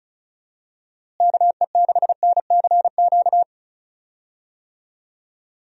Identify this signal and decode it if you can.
Morse code, heard as a keyed tone, about 36 wpm, 700 Hz. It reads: KE6NCQ
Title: USA Call Signs - 6 Characters - Repeated 2x - 35wpm